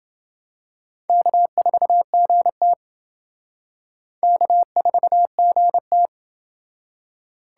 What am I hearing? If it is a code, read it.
Morse code, heard as a keyed tone, about 28 wpm, 700 Hz. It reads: K4GT K4GT